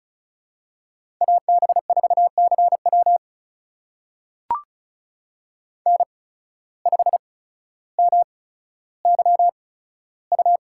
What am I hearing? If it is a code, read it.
Morse code, heard as a keyed tone, about 34 wpm, spaced as for 15 wpm, 700 Hz. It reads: AB4CW N5MYU
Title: USA Call Signs - 8x Character Spacing - 35wpm